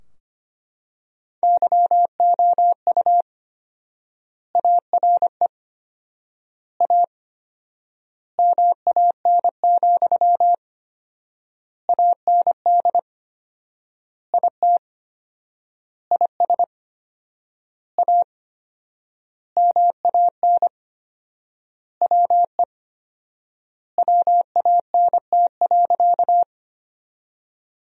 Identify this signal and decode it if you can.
Morse code, heard as a keyed tone, about 25 wpm, 700 Hz. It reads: YOU ARE A MAN, AND IT IS A MAN WE WANT.
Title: 4x Word-Spacing - Sentences from Top 200 Words - 25wpm